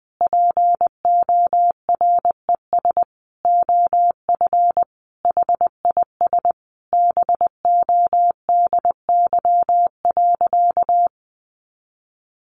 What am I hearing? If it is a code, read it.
Morse code, heard as a keyed tone, about 20 wpm, 700 Hz. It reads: PORES OF HIS BODY.
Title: A Sound Of Thunder 20wpm